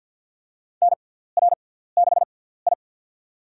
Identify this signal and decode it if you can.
Morse code, heard as a keyed tone, about 51 wpm, spaced as for 17 wpm, 700 Hz. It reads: NR6I
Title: USA Call Signs - 4 Characters - 6x Character Spacing - 50wpm